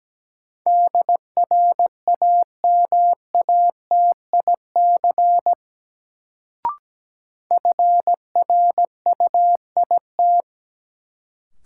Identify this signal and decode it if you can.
Morse code, heard as a keyed tone, about 17 wpm, 700 Hz. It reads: DRAMATIC FRUIT
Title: Sentences from Top 1300 Words - New Word Drill Alt - 17wpm